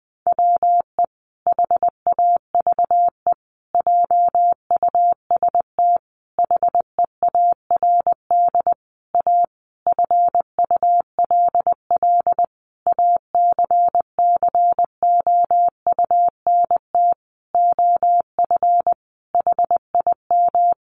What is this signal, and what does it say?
Morse code, heard as a keyed tone, about 20 wpm, 700 Hz. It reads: WE HAVE JUST HEARD A FULL ACCOUNT OF HIM
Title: Sentences from Top 900 Words 20wpm